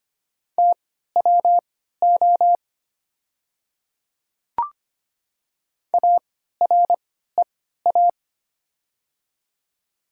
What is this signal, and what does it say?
Morse code, heard as a keyed tone, about 25 wpm, spaced as for 14 wpm, 690 Hz. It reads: TWO AREA
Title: 3x Character Spacing - Top 500 Words - 25wpm